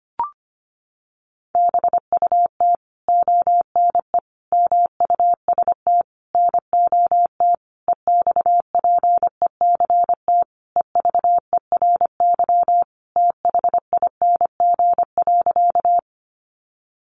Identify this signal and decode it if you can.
Morse code, heard as a keyed tone, about 25 wpm, 700 Hz. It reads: BUT ONE MUST NOT EXPECT EVERYTHING.
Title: Sentences from Top 1100 Words 25wpm